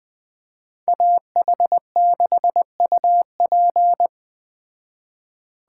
Morse code, heard as a keyed tone, about 20 wpm, 700 Hz. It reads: AH6UP
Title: Speed Racing - USA 2x2 Call Signs 17wpm